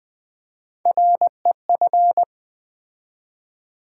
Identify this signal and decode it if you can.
Morse code, heard as a keyed tone, about 20 wpm, 700 Hz. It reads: REF
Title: Build A Word - 3 Letters At A Time - 6-Letter Words - 20wpm